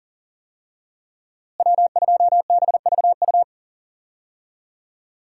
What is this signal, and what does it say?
Morse code, heard as a keyed tone, about 40 wpm, 700 Hz. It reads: W2BVU
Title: USA Call Signs - 5 Characters - Repeated 2x - 40wpm